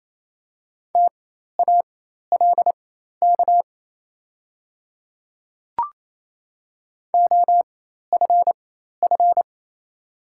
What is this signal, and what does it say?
Morse code, heard as a keyed tone, about 28 wpm, spaced as for 13 wpm, 700 Hz. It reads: TALK OFF
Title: 4x Character Spacing - Top 500 Words - 28wpm